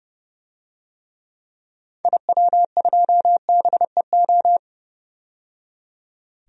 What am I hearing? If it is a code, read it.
Morse code, heard as a keyed tone, about 30 wpm, 700 Hz. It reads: IW2BEO
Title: DXCC-100 Call Signs - 6 Characters - 30wpm